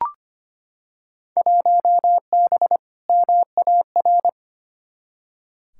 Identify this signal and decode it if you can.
Morse code, heard as a keyed tone, about 25 wpm, 700 Hz. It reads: 1B MAR
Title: Field Day Exchanges Encore - 25wpm